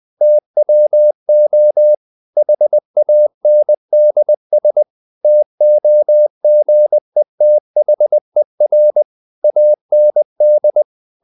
Morse code, heard as a keyed tone, about 20 wpm, 600 Hz. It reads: TWO HANDS TOGETHER AND
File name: the old man and the sea part 2 20wpm